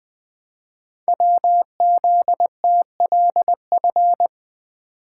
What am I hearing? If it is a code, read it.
Morse code, heard as a keyed tone, about 20 wpm, 700 Hz. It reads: WZTLF
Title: Code Groups - 5 Letters - 20wpm